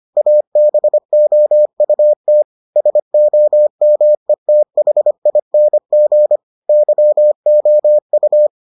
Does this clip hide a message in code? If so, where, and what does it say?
Yes: Morse code from 0.2 to 8.5 s, heard as a keyed tone, about 25 wpm, 600 Hz. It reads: ABOUT SOMETHING YOU